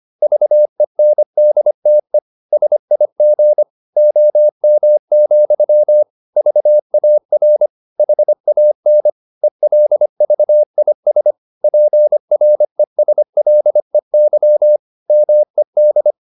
Morse code, heard as a keyed tone, about 25 wpm, 600 Hz. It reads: VENDTE SIG OM, VAR HAN ELVIS PRESLEY MED
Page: Ferie 25 wpm